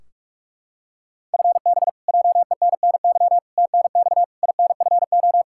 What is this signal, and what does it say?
Morse code, heard as a keyed tone, about 46 wpm, 700 Hz. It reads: FB JENNY TNX INFO
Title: Practice QSO - Exchange 3 - More Info - 45wpm